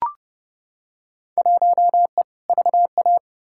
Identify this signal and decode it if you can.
Morse code, heard as a keyed tone, about 30 wpm, 700 Hz. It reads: 1E VA
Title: Field Day Exchanges - 30wpm